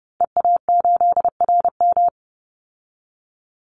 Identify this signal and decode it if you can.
Morse code, heard as a keyed tone, about 30 wpm, 700 Hz. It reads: EA8RM
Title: Call Signs Repeated 2x - 30wpm